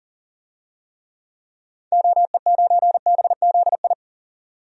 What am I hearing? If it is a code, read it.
Morse code, heard as a keyed tone, about 40 wpm, 700 Hz. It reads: OE9BZI
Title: DXCC-100 Call Signs - 6 Characters - 40wpm